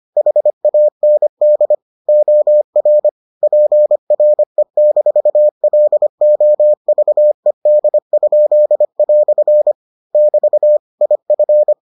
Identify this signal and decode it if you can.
Morse code, heard as a keyed tone, about 25 wpm, 600 Hz. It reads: HAND OR PRE-LOVED?" <BT> IF